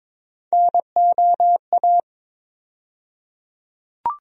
Morse code, heard as a keyed tone, about 22 wpm, 700 Hz. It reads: NOA
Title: USA Call Sign Postfixes - 3 Letters 22wpm